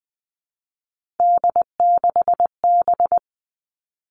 Morse code, heard as a keyed tone, about 19 wpm, 700 Hz. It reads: D6B
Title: Three Challenging Letter-Number Combinations 20wpm